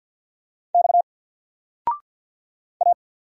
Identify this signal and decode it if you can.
Morse code, heard as a keyed tone, about 50 wpm, 700 Hz. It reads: X A
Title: Single Letter-Number 50wpm